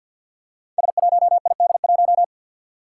Morse code, heard as a keyed tone, about 51 wpm, 700 Hz. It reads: S1ID1
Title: Five Challenging Letter-Number Combinations 50wpm